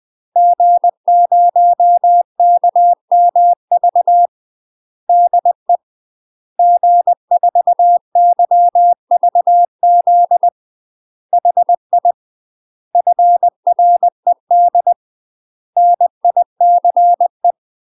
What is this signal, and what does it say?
Morse code, heard as a keyed tone, about 20 wpm, 700 Hz. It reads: G0KMV DE G4YVZ HI FRED NICE
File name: UK MASTER QSOs TRAINING / 20WPM / UK MASTER QSOs 0037